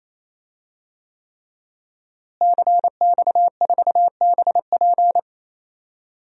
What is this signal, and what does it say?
Morse code, heard as a keyed tone, about 28 wpm, 700 Hz. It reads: CX4BP
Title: DXCC-100 Call Signs - 5 Characters - Repeated 3x - 28wpm